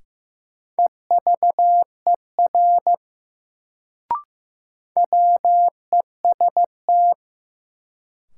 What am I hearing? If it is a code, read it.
Morse code, heard as a keyed tone, about 15 wpm, 700 Hz. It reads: EVER WEST
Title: Sentences from Top 400 Words - New Word Drill - 15wpm